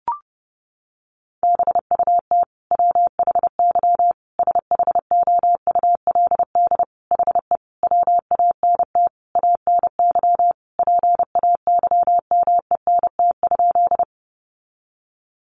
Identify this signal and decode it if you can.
Morse code, heard as a keyed tone, about 30 wpm, 700 Hz. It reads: BUT WHY SHOULD HE WANT ANY PAYMENT?
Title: Sentences from Top 1200 Words 30wpm